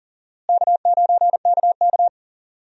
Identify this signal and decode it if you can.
Morse code, heard as a keyed tone, about 37 wpm, 700 Hz. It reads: K9KK
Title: USA Call Signs - 4 Characters - Repeated 3x - 40wpm